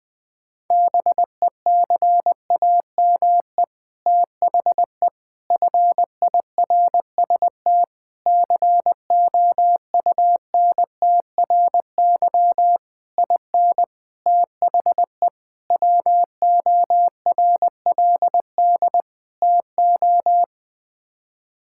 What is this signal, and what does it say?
Morse code, heard as a keyed tone, about 20 wpm, 700 Hz. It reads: BECAME THE FIRST COUNTRY IN THE WORLD TO
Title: Sets of 8 Words - 20wpm